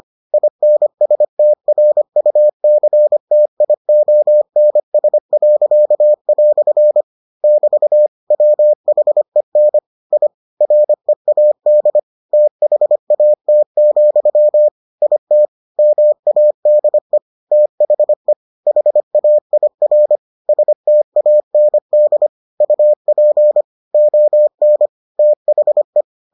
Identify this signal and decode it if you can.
Morse code, heard as a keyed tone, about 25 wpm, 600 Hz. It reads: INSTRUCTIONS." <BT> WHEN I READ THAT, IT MADE THE HAIR STAND UP ON THE